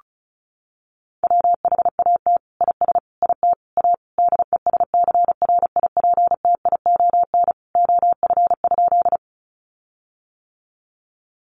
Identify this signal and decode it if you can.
Morse code, heard as a keyed tone, about 34 wpm, 700 Hz. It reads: WHAT IS IT A DESCRIPTION OF?
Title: Sentences from Top 1300 Words 35wpm